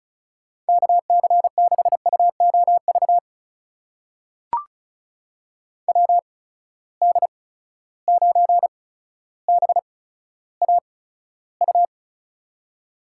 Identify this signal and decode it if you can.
Morse code, heard as a keyed tone, about 36 wpm, spaced as for 15 wpm, 700 Hz. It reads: KC6UOV WD9BAU
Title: USA Call Signs - 6 Characters - 8x Character Spacing - 35wpm